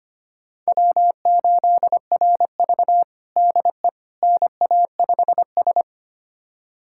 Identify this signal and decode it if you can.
Morse code, heard as a keyed tone, about 25 wpm, 700 Hz. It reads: W8RV DE NA5S
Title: USA Callsign DE Callsign - 4 Characters - 25wpm